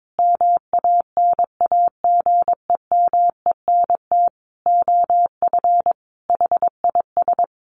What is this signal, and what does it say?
Morse code, heard as a keyed tone, about 22 wpm, 700 Hz. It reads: MANAGEMENT OF HIS
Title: Sentences from Top 1200 Words 22wpm